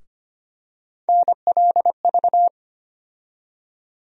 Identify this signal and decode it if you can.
Morse code, heard as a keyed tone, about 25 wpm, 700 Hz. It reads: NLV
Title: USA Call Sign Postfixes - 3 Letters 25wpm